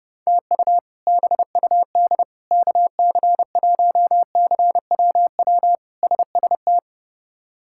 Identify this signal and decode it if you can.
Morse code, heard as a keyed tone, about 32 wpm, 700 Hz. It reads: TU BUD KC1CWW SST
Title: K1USN SST QSO Practice - 30wpm